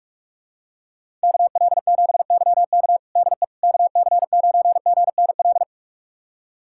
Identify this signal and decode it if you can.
Morse code, heard as a keyed tone, about 41 wpm, 700 Hz. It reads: KL7YK DE KC9GNL
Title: USA Callsign DE Callsign 45wpm